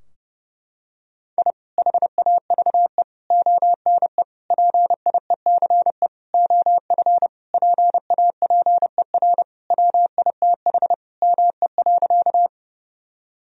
Morse code, heard as a keyed tone, about 29 wpm, 700 Hz. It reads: I HAVE ONE PIECE OF PAPER WITH ME.
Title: Sentences from Top 1000 Words - Review Part 2 - 30wpm